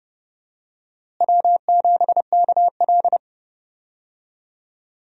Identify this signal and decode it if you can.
Morse code, heard as a keyed tone, about 31 wpm, 700 Hz. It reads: W7KL
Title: USA Call Signs - 4 Characters - Repeated 2x - 30wpm